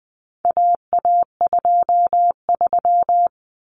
Morse code, heard as a keyed tone, about 20 wpm, 700 Hz. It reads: AA23
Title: Common Cut Numbers - 4-Digit 20wpm